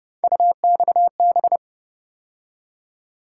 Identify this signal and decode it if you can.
Morse code, heard as a keyed tone, about 29 wpm, 700 Hz. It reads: UXB